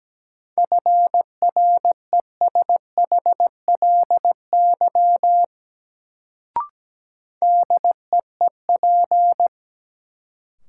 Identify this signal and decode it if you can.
Morse code, heard as a keyed tone, about 17 wpm, 700 Hz. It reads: FRESHLY DEEP
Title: Sets of 5 Words Encore - New Word Drill - 17wpm